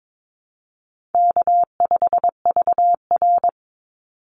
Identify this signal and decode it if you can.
Morse code, heard as a keyed tone, about 22 wpm, 700 Hz. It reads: K5VR
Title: Call Signs Repeated 2x - 22wpm